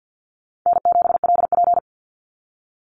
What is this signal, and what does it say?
Morse code, heard as a keyed tone, about 50 wpm, 700 Hz. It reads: N7LP
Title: NA SOTA Chaser Callsigns 50wpm